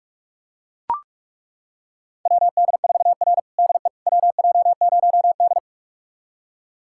Morse code, heard as a keyed tone, about 42 wpm, 700 Hz. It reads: WD4R DE WJ0D
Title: USA Callsign DE Callsign - 4 Characters - 45wpm